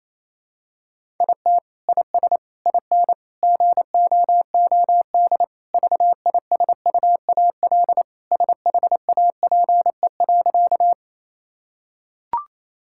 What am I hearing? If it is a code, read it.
Morse code, heard as a keyed tone, about 28 wpm, 700 Hz. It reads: IT IS IN GOOD VISUAL SHAPE.
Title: Sentences from Top 1000 Words - Review Part 6 - 28wpm